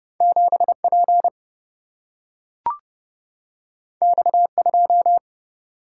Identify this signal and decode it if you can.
Morse code, heard as a keyed tone, about 30 wpm, 700 Hz. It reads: 7 P X 2
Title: Single-Letter Single-Number 30wpm